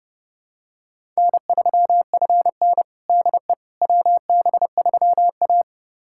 Morse code, heard as a keyed tone, about 32 wpm, 700 Hz. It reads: N3FN DE WB3A
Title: USA Callsign DE Callsign - 4 Characters - 30wpm